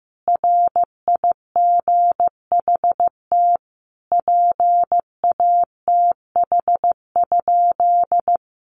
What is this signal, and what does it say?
Morse code, heard as a keyed tone, about 15 wpm, 700 Hz. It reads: RIGHT PATH?
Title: Sentences from Top 1400 Words 15wpm